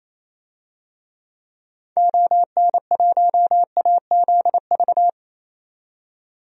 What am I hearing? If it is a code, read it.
Morse code, heard as a keyed tone, about 28 wpm, 700 Hz. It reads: ON1AZV
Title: DXCC-100 Call Signs - 6 Characters - 28wpm